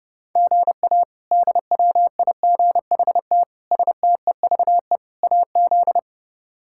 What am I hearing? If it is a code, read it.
Morse code, heard as a keyed tone, about 30 wpm, 700 Hz. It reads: GA DWIGHT STEVE AZ
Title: K1USN SST QSO Practice - 30wpm